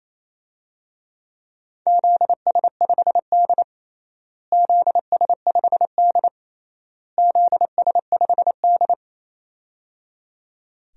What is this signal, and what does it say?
Morse code, heard as a keyed tone, about 28 wpm, 700 Hz. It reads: ZS5D ZS5D ZS5D
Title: DXCC-100 Call Signs - 4 Characters - Repeated 3x - 28wpm